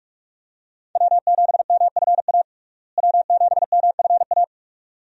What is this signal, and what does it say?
Morse code, heard as a keyed tone, about 46 wpm, 700 Hz. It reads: W7MFA W7MFA
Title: USA Call Signs - 5 Characters - Repeated 2x - 45wpm